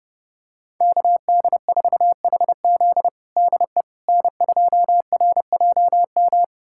Morse code, heard as a keyed tone, about 30 wpm, 700 Hz. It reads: KD4HZ DE N2RJM
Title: USA Callsign DE Callsign - 5 Characters - 30wpm